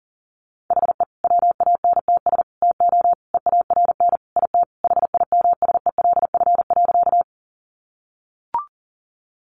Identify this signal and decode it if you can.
Morse code, heard as a keyed tone, about 40 wpm, 700 Hz. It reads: HE WANTS TO EARN IT HIMSELF.
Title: Additional Easy Phrases 40wpm